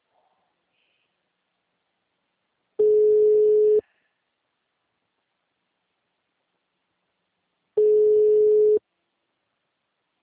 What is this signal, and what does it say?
Ringback tone (Europe, 425 Hz).